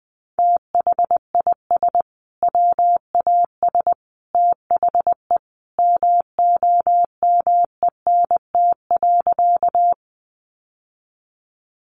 Morse code, heard as a keyed tone, about 20 wpm, 700 Hz. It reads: THIS WAS THE MOMENT.
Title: Sentences from Top 1000 Words 20wpm